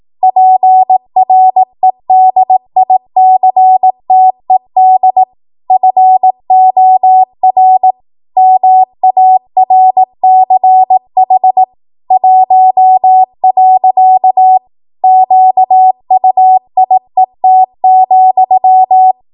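Morse code, heard as a keyed tone, about 18 wpm, 750 Hz. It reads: PREDICTED FOR MARCH 1. QUIET,
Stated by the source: Code practice at 18 WPM, a transition run bridging 15 to 20 WPM.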